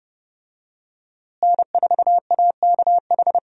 Morse code, heard as a keyed tone, about 30 wpm, 700 Hz. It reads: N4AKH
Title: USA Call Signs - 5 Characters - Repeated 3x - 30wpm